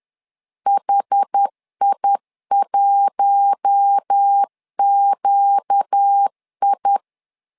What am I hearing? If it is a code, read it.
Morse code, heard as a keyed tone, about 10 wpm, 800 Hz. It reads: HI1QI
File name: LEZIONI / 4 Q1 10wpm 2448